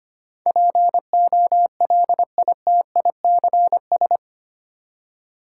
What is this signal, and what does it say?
Morse code, heard as a keyed tone, about 25 wpm, 690 Hz. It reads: POLITICS
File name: 8 Letter Words - First 3 Letters Given - 25wpm